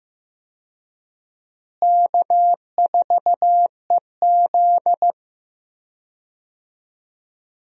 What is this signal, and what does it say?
Morse code, heard as a keyed tone, about 15 wpm, 700 Hz. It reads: K4EZ
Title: USA Call Signs - 4 Characters - Repeated 3x - 15wpm